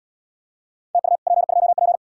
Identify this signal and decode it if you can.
Morse code, heard as a keyed tone, about 65 wpm, 700 Hz. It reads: TU P2P